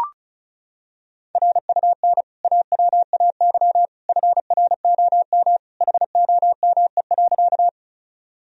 Morse code, heard as a keyed tone, about 36 wpm, 700 Hz. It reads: RUN AWAY FROM HOME.
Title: Sentences from Top 1000 Words - Review Part 1 - 35wpm